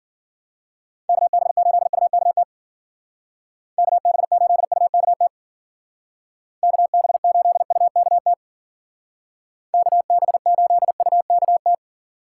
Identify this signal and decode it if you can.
Morse code, heard as a keyed tone, about 53 wpm, 700 Hz. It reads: KB8UKT KB8UKT KB8UKT KB8UKT